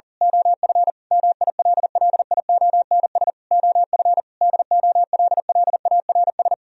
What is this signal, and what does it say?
Morse code, heard as a keyed tone, about 38 wpm, 700 Hz. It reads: OF MILLIONS OF DOLLARS
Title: Speed-Racing - Sets of 5 Words 40wpm